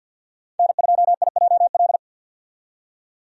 Morse code, heard as a keyed tone, about 47 wpm, 700 Hz. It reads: N2IJL
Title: USA Call Signs - 5 Characters - 50wpm